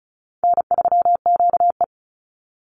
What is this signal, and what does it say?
Morse code, heard as a keyed tone, about 34 wpm, 700 Hz. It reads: N3QE
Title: Easy Call Signs Repeated 3x - 35wpm